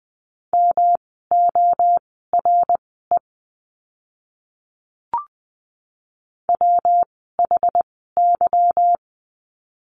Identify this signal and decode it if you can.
Morse code, heard as a keyed tone, about 20 wpm, spaced as for 15 wpm, 700 Hz. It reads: MORE WHY
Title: 244 Lesson 39f - Letter z - Words 20wpm